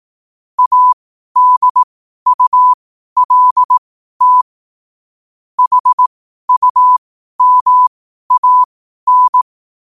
Morse code, heard as a keyed tone, about 18 wpm, spaced as for 12 wpm, 1000 Hz. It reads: ADULT HUMAN
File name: qotdfiles / 12wpm